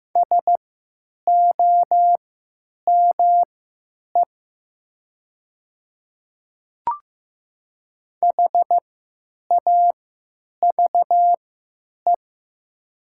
Morse code, heard as a keyed tone, about 15 wpm, spaced as for 9 wpm, 700 Hz. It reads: SOME HAVE
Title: 3x Character Spacing - Top 200 Words - 15wpm